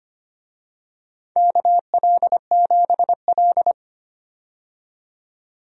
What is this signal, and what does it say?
Morse code, heard as a keyed tone, about 25 wpm, 700 Hz. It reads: KL7L
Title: DXCC-100 Call Signs - 4 Characters - 25wpm